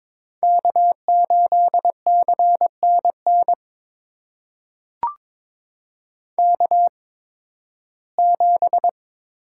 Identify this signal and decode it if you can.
Morse code, heard as a keyed tone, about 22 wpm, spaced as for 14 wpm, 700 Hz. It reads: K8CNN K7
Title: USA Call Signs - 5 Characters - 8x Character Spacing - 22wpm